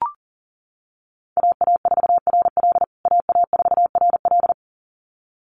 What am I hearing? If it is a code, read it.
Morse code, heard as a keyed tone, about 40 wpm, 700 Hz. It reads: AA4RL AA4RL
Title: USA Call Signs Repeated 2x 40wpm